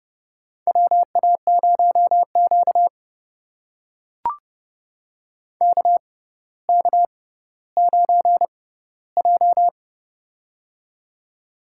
Morse code, heard as a keyed tone, about 31 wpm, spaced as for 16 wpm, 700 Hz. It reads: WA0Q KK9J
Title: USA Call Signs - 4 Characters - 6x Character Spacing - 30wpm